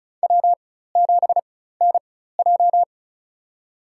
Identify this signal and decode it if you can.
Morse code, heard as a keyed tone, about 34 wpm, spaced as for 16 wpm, 700 Hz. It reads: W7NJ
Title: USA Call Signs - 4 Characters - 4x Character Spacing - 35wpm